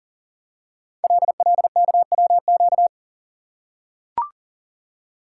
Morse code, heard as a keyed tone, about 42 wpm, 690 Hz. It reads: LLKWQ
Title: Code Groups - 5 Letter:Numbers - 40wpm